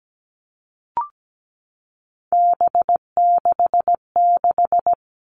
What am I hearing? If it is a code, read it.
Morse code, heard as a keyed tone, about 17 wpm, 700 Hz. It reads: B66
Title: All Cut Numbers - 3-Digit 17wpm